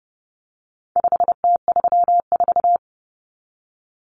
Morse code, heard as a keyed tone, about 30 wpm, 700 Hz. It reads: 5T34
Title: Common Cut Numbers - 4-Digit 30wpm